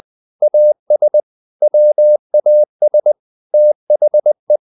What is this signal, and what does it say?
Morse code, heard as a keyed tone, about 20 wpm, 600 Hz. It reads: AS WAS THE
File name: the old man and the sea part 2 20wpm